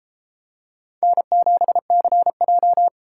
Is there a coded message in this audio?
Yes: Morse code from 1.0 to 2.9 s, heard as a keyed tone, about 29 wpm, 700 Hz. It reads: N7CJ